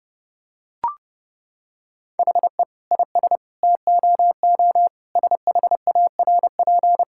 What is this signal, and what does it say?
Morse code, heard as a keyed tone, about 30 wpm, 700 Hz. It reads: HE IS TOO SHARP
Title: Sentences from Top 1600 Words 30wpm